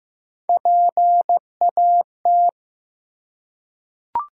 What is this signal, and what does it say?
Morse code, heard as a keyed tone, about 16 wpm, 700 Hz. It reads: PAT
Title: USA Common Names - Max 5 Letters - 15wpm